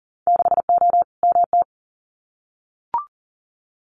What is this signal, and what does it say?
Morse code, heard as a keyed tone, about 40 wpm, 700 Hz. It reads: 6O MT
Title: Winter Field Day Exchanges Encore 40wpm